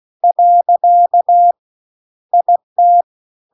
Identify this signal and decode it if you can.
Morse code, heard as a keyed tone, about 16 wpm, 700 Hz. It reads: . IT
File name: CW-The-Big-Twoheaded-River-2-16WPM